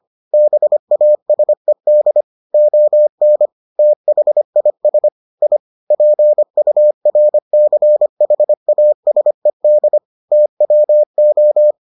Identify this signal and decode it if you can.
Morse code, heard as a keyed tone, about 25 wpm, 600 Hz. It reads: BASED ON THIS I PURCHASED TWO